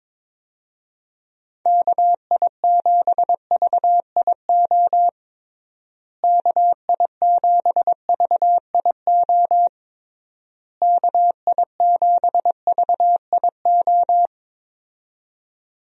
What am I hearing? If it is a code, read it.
Morse code, heard as a keyed tone, about 23 wpm, 700 Hz. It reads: KI7VIO KI7VIO KI7VIO
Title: USA Call Signs - 6 Characters - Repeated 3x - 22wpm